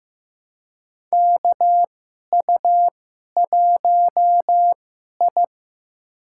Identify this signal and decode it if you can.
Morse code, heard as a keyed tone, about 15 wpm, spaced as for 11 wpm, 700 Hz. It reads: KU1I
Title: USA Call Signs - 2x Character Spacing - 15wpm